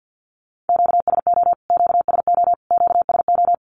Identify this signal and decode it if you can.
Morse code, heard as a keyed tone, about 52 wpm, 700 Hz. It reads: QSO QSO QSO
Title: Q-Codes Repeated 3x 50wpm